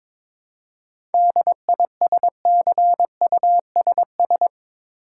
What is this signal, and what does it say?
Morse code, heard as a keyed tone, about 22 wpm, 700 Hz. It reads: DISCUSS
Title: Seven Letter Words - 22wpm